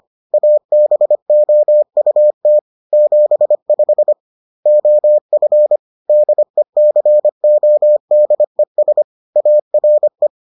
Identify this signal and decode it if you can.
Morse code, heard as a keyed tone, about 25 wpm, 600 Hz. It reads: ABOUT 75 OF DECODES ARE